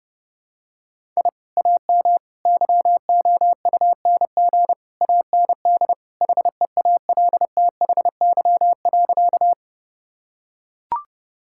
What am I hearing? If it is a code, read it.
Morse code, heard as a keyed tone, about 30 wpm, 700 Hz. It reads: I AM YOUNG AND HEALTHY.
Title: Sentences from Top 1000 Words - Review Part 5 - 30wpm